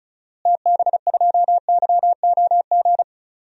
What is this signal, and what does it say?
Morse code, heard as a keyed tone, about 36 wpm, 700 Hz. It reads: TB2YOG
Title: DXCC-100 Call Signs - 6 Characters - Repeated 2x - 35wpm